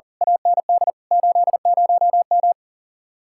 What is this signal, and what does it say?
Morse code, heard as a keyed tone, about 39 wpm, 700 Hz. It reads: AND 80M
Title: Speed-Racing Common QSO Phrases Encore - 40wpm